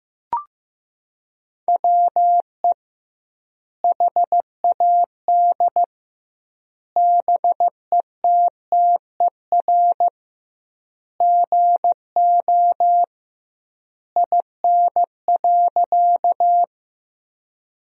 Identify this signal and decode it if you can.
Morse code, heard as a keyed tone, about 15 wpm, 700 Hz. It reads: WE HAD BETTER GO IN.
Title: 2x Word-Spacing - Sentences from Top 400 Words - 15wpm